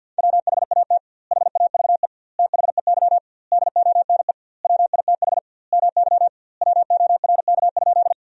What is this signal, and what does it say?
Morse code, heard as a keyed tone, about 51 wpm, 700 Hz. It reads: WHAT HAVE THEY DONE WITH MY WORK?
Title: 2x Word-Spacing - Sentences from Top 500 Words - 50wpm